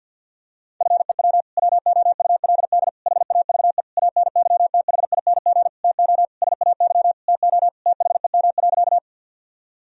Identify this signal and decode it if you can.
Morse code, heard as a keyed tone, about 52 wpm, 700 Hz. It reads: FEW WOULD HAVE ANYTHING TO SAY TO THEM.
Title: Sentences from Top 1000 Words - Review Part 1 - 50wpm